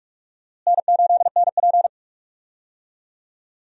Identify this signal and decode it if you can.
Morse code, heard as a keyed tone, about 47 wpm, 700 Hz. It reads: N8NP
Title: USA Call Signs - 4 Characters - Repeated 2x - 45wpm